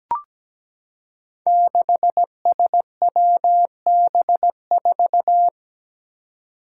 Morse code, heard as a keyed tone, about 17 wpm, 700 Hz. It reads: 6SWB4
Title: Five Challenging Letter-Number Combinations 17wpm